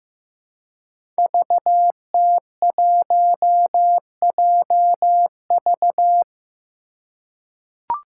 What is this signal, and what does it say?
Morse code, heard as a keyed tone, about 15 wpm, 700 Hz. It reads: VT1JV
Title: Five Challenging Letter-Number Combinations 15wpm